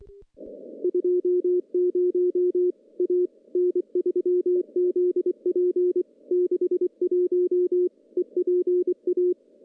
Morse code, heard as a keyed tone, about 23 wpm, 360 Hz. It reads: 20 A N3ZP 61 EPA
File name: CW-SS-k5tr